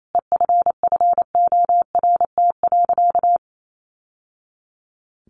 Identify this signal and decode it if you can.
Morse code, heard as a keyed tone, about 28 wpm, 700 Hz. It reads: EFFORT.
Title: Sentences from Top 1200 Words 28wpm